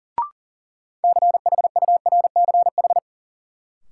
Morse code, heard as a keyed tone, about 38 wpm, 700 Hz. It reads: CHURCH
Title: Sets of 5 Words - New Word Drill - 40wpm